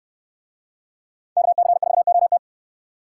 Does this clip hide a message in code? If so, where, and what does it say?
Yes: Morse code from 1.4 to 2.4 s, heard as a keyed tone, about 69 wpm, 700 Hz. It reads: KC4OT